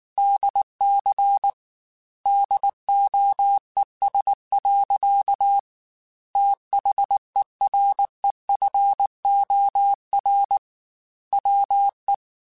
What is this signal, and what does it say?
Morse code, heard as a keyed tone, about 19 wpm, 800 Hz. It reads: DC DOES. THEREFOR WE
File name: Technical Subjects / Technical Subjects-mp3s / Sw Pwr Sup @18